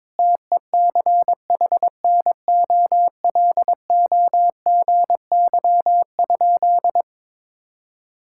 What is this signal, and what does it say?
Morse code, heard as a keyed tone, about 22 wpm, 700 Hz. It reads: TECHNOLOGY?
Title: Sentences from Top 1000 Words - Review Part 5 - 22wpm